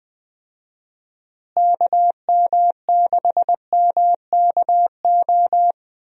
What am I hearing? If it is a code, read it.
Morse code, heard as a keyed tone, about 21 wpm, 700 Hz. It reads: KM6MKO
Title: USA Call Signs - 6 Characters - Repeated 3x - 20wpm